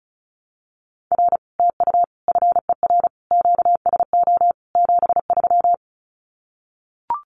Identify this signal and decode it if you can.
Morse code, heard as a keyed tone, about 34 wpm, 700 Hz. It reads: R TU FER QSO 73
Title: Common QSO Phrases Encore 35wpm